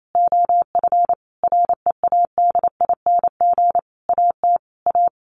Morse code, heard as a keyed tone, about 28 wpm, 690 Hz. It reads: OF READING AT A
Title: Sentences from Top 900 Words 28wpm